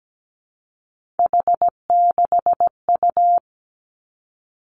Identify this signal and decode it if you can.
Morse code, heard as a keyed tone, about 17 wpm, 700 Hz. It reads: H6U
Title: Three Challenging Letter-Number Combinations 17wpm